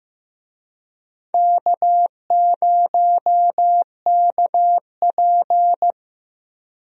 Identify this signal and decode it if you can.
Morse code, heard as a keyed tone, about 15 wpm, 700 Hz. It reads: K0KP
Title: USA Call Signs - 4 Characters - Repeated 3x - 15wpm